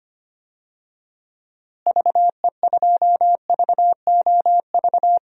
Morse code, heard as a keyed tone, about 25 wpm, 700 Hz. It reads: VE2VOV
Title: DXCC-100 Call Signs - 6 Characters - Repeated 3x - 25wpm